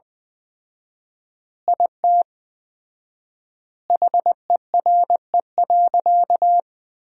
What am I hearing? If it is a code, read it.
Morse code, heard as a keyed tone, about 20 wpm, 700 Hz. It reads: IT HERE.
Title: 4x Word-Spacing - Sentences from Top 400 Words - 20wpm